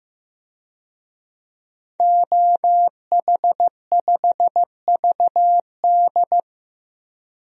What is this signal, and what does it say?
Morse code, heard as a keyed tone, about 15 wpm, 700 Hz. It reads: OH5VD
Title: DXCC-100 Call Signs - 5 Characters - 15wpm